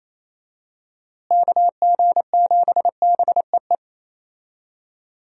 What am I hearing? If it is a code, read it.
Morse code, heard as a keyed tone, about 28 wpm, 700 Hz. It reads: KG7BEE
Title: USA Call Signs - 6 Characters - Repeated 3x - 28wpm